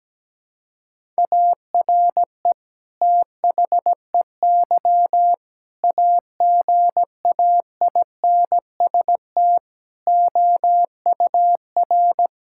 Morse code, heard as a keyed tone, about 17 wpm, 700 Hz. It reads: ARE THEY AGAINST OUR
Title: Sentences from Top 1000 Words - Review Part 5 - 17wpm